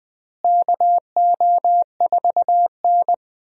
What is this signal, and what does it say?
Morse code, heard as a keyed tone, about 20 wpm, 700 Hz. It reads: KO4N
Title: USA Call Signs - 4 Characters - Repeated 3x - 20wpm